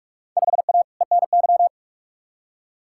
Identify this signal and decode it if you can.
Morse code, heard as a keyed tone, about 45 wpm, 700 Hz. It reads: 5A ENY
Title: Field Day Exchanges Encore - 45wpm